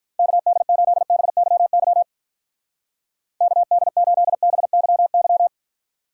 Morse code, heard as a keyed tone, about 43 wpm, 700 Hz. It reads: KD8BYY KD8BYY
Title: Speed Racing - USA 2x3 Call Signs 35wpm